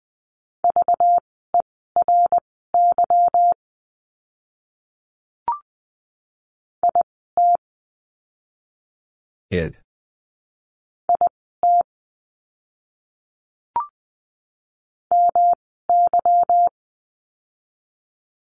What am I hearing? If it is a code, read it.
Morse code, heard as a keyed tone, about 20 wpm, spaced as for 15 wpm, 700 Hz. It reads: VERY IT IT MY
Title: 244 Lesson 39f - Letter z - Words 20wpm